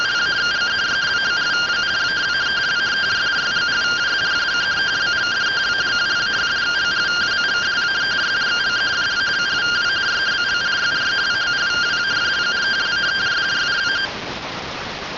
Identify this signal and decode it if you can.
Two-tone teleprinter signal, about 170 Hz shift.